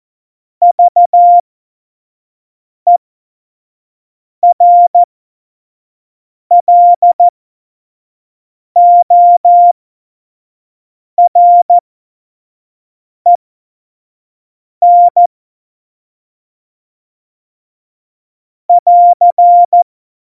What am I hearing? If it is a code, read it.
Morse code, heard as a keyed tone, about 14 wpm, spaced as for 5 wpm, 700 Hz. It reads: VERLOREN <AR>
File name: Morsekurs / Witze / Witz 02. Oktober 2018, 25 F70